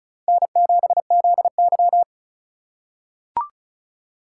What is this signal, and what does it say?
Morse code, heard as a keyed tone, about 36 wpm, 700 Hz. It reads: N7ZY
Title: USA Call Signs - 2x Character Spacing - 35wpm